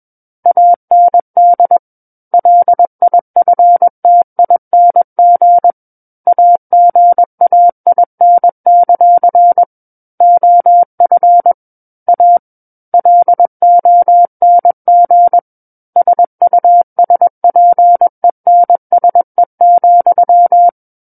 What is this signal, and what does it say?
Morse code, heard as a keyed tone, about 21 wpm, 700 Hz. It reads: AND LIFTING AGAIN; OF A LONG SUSPENSE,
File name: Book One / 12-Chapter 12 - 21 WPM